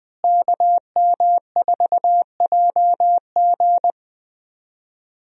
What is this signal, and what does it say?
Morse code, heard as a keyed tone, about 20 wpm, 700 Hz. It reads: KM4JG
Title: USA Call Signs - 5 Characters - Repeated 2x - 20wpm